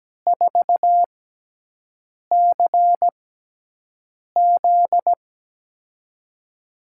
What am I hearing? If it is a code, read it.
Morse code, heard as a keyed tone, about 17 wpm, spaced as for 6 wpm, 700 Hz. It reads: 4CZ
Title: USA Call Signs - 4 Characters - 6x Character Spacing - 17wpm